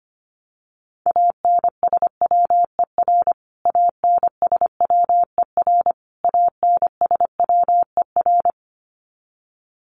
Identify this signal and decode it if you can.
Morse code, heard as a keyed tone, about 25 wpm, 700 Hz. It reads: ANSWER ANSWER ANSWER
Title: Top 500 Words Repeated 25wpm